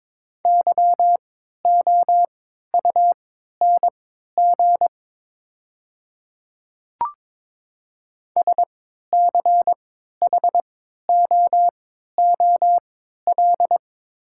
Morse code, heard as a keyed tone, about 22 wpm, spaced as for 13 wpm, 700 Hz. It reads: YOUNG SCHOOL
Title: 3x Character Spacing - Top 500 Words - 22wpm